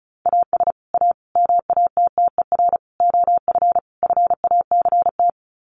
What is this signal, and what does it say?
Morse code, heard as a keyed tone, about 34 wpm, 700 Hz. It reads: AS A MATTER OF FACT
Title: Sets of 5 Words 35wpm